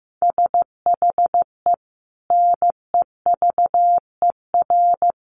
Morse code, heard as a keyed tone, about 15 wpm, 700 Hz. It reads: SHE NEVER
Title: Sentences from Top 1400 Words 15wpm